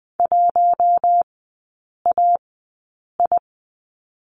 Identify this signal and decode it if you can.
Morse code, heard as a keyed tone, about 19 wpm, spaced as for 8 wpm, 700 Hz. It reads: 1AI
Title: 041 Lesson 8e - Number 1 - Three Characters 20wpm